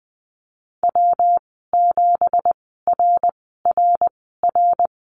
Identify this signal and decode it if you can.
Morse code, heard as a keyed tone, about 20 wpm, spaced as for 15 wpm, 700 Hz. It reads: W7RRR
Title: 193 Lesson 31g - Letter v - Call Signs 20wpm